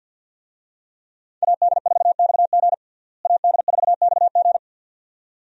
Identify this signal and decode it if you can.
Morse code, heard as a keyed tone, about 51 wpm, 700 Hz. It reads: AD4XG AD4XG
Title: USA Call Signs - 5 Characters - Repeated 2x - 50wpm